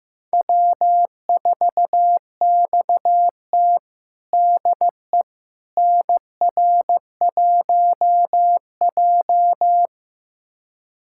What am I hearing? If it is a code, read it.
Morse code, heard as a keyed tone, about 15 wpm, 700 Hz. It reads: W4XT DE NR1J
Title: USA Callsign DE Callsign - 4 Characters - 15wpm